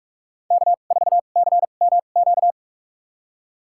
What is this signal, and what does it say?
Morse code, heard as a keyed tone, about 43 wpm, spaced as for 32 wpm, 690 Hz. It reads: K4CMQ
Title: USA Call Signs - 5 Characters - 2x Character Spacing - 45wpm